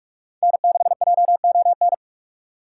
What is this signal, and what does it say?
Morse code, heard as a keyed tone, about 49 wpm, 700 Hz. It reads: N6JON